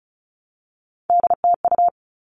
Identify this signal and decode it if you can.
Morse code, heard as a keyed tone, about 34 wpm, 700 Hz. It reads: DTU
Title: All Cut Numbers - 3-Digit 35wpm